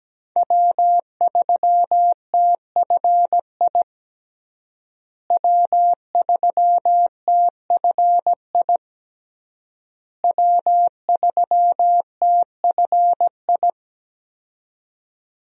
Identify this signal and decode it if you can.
Morse code, heard as a keyed tone, about 17 wpm, 700 Hz. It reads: W3TFI W3TFI W3TFI
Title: USA Call Signs - 5 Characters - Repeated 3x - 17wpm